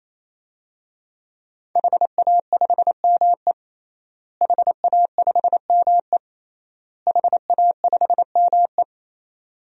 Morse code, heard as a keyed tone, about 28 wpm, 700 Hz. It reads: HA5ME HA5ME HA5ME
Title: DXCC-100 Call Signs - 5 Characters - Repeated 3x - 28wpm